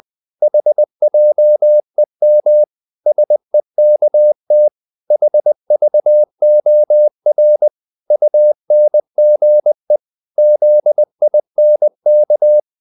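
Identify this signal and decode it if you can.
Morse code, heard as a keyed tone, about 20 wpm, 600 Hz. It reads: HJEM SEKT HVOR UNGE ZINK
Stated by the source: Morse code practice of short Danish words at 20 wpm, from CW med Gnister.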